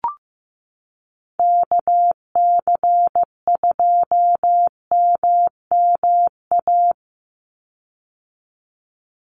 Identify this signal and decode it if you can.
Morse code, heard as a keyed tone, about 15 wpm, 700 Hz. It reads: KC2MMA
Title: USA Call Signs 15wpm